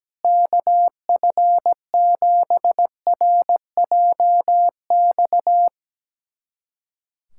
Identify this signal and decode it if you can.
Morse code, heard as a keyed tone, about 17 wpm, 700 Hz. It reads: KF7RJX
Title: USA Callsign DE Callsign - 6 Characters - 17wpm